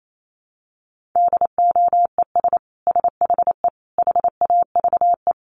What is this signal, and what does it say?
Morse code, heard as a keyed tone, about 28 wpm, 700 Hz. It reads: DOES SHE HAVE
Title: Sentences from Top 1400 Words 28wpm